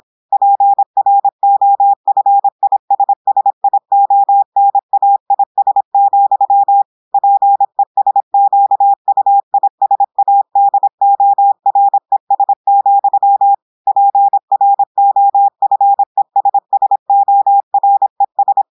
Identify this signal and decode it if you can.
Morse code, heard as a keyed tone, about 26 wpm, 800 Hz. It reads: PROFISSIONAIS, PESQUISADORES, PROFESSORES
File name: audios / portugais / portugais texte en clair 1 vitesse 26 mots minutes